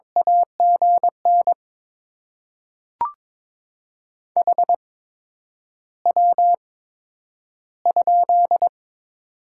Morse code, heard as a keyed tone, about 23 wpm, spaced as for 10 wpm, 700 Hz. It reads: AGN HW?
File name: Top 20 QSO Elements - 8x Character Spacing - 22wpm